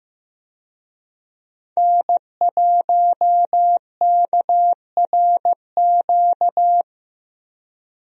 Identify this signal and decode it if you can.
Morse code, heard as a keyed tone, about 15 wpm, 700 Hz. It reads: N1KRQ
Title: USA Call Signs - 5 Characters - 15wpm